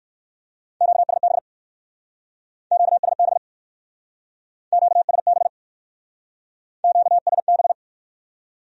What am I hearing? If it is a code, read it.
Morse code, heard as a keyed tone, about 57 wpm, 700 Hz. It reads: QSB QSB QSB QSB